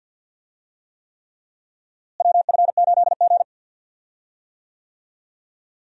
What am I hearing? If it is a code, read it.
Morse code, heard as a keyed tone, about 53 wpm, 700 Hz. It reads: WF8G
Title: USA Call Signs - 4 Characters - Repeated 3x - 50wpm